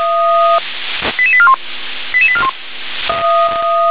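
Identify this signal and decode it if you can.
Five-tone selective call (ZVEI), heard as quick runs of five tones, tones of about 70 ms. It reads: 88841 88841
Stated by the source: ZVEI five-tone selective call received by SDR.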